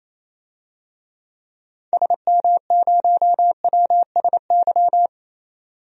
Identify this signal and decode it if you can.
Morse code, heard as a keyed tone, about 28 wpm, 700 Hz. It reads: SM0WSY
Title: DXCC-100 Call Signs - 6 Characters - 28wpm